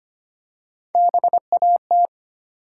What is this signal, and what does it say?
Morse code, heard as a keyed tone, about 25 wpm, 700 Hz. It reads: BAT
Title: Three Letter Words - 25wpm